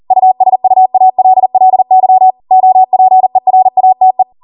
Morse code, heard as a keyed tone, about 40 wpm, 750 Hz. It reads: USUALLY OPERATE